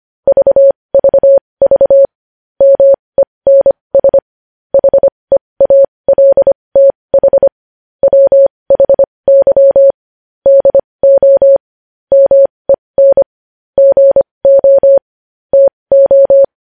Morse code, heard as a keyed tone, about 25 wpm, spaced as for 20 wpm, 550 Hz. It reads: VVV MENS HEALTH WHY DO MEN GO TO
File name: podcast / News headlines-1x-25x20